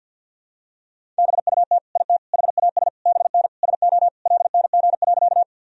Morse code, heard as a keyed tone, about 48 wpm, 700 Hz. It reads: BUT IT HAS BN SO LNG.
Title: Sentences from Top 100 Words and Abbreviations 50wpm